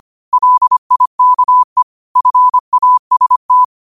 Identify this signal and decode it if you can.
Morse code, heard as a keyed tone, about 25 wpm, 1000 Hz. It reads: LIKE FAST
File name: qotdfiles / 25wpm